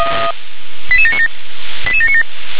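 Five-tone selective call (ZVEI), heard as quick runs of five tones, tones of about 70 ms. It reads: 88797 88797
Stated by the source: Five-tone ZVEI selective call, received by SDR.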